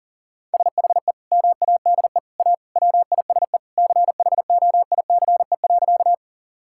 Morse code, heard as a keyed tone, about 37 wpm, 700 Hz. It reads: SHE MADE A WISE CHOICE.
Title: Sentences from Top 1600 Words 40wpm